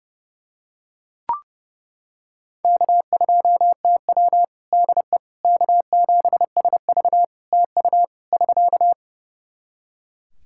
Morse code, heard as a keyed tone, about 32 wpm, 700 Hz. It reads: K2TW DE K7SV TU <SK>
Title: K1USN SST QSO Practice 30wpm